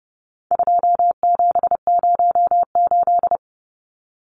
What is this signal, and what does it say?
Morse code, heard as a keyed tone, about 30 wpm, 700 Hz. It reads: 2708
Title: Numbers - 4-Digit 30wpm